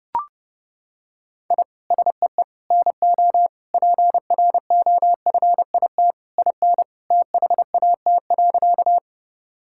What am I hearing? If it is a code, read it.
Morse code, heard as a keyed tone, about 30 wpm, 700 Hz. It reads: I SEE NO PROFIT IN THAT.
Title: Sentences from Top 1000 Words - Review Part 5 - 30wpm